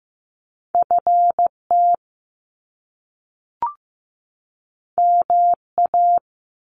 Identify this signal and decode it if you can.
Morse code, heard as a keyed tone, about 15 wpm, 700 Hz. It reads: FT MA
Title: Top Two Letter Combinations in Words 15wpm